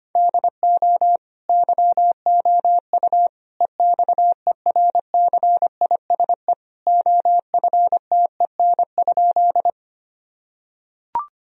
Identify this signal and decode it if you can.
Morse code, heard as a keyed tone, about 25 wpm, 700 Hz. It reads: DO YOU EXERCISE OFTEN?
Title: Sentences from Top 1000 Words - Review Part 4 - 25wpm